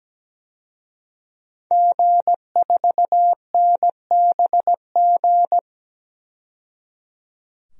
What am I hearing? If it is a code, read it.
Morse code, heard as a keyed tone, about 17 wpm, 700 Hz. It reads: G4NBG
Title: DXCC-100 Call Signs - 5 Characters - 17wpm